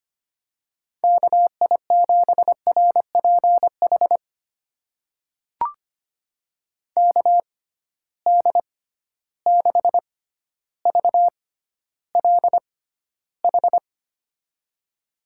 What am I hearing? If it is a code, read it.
Morse code, heard as a keyed tone, about 25 wpm, spaced as for 13 wpm, 700 Hz. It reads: KI7RPH KD6VLH
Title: USA Call Signs - 6 Characters - 6x Character Spacing - 25wpm